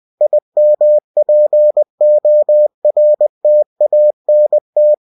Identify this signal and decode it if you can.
Morse code, heard as a keyed tone, about 20 wpm, 600 Hz. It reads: IMPORTANT
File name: cw-books / winnie-the-pooh / winnie-20wpm-page-0009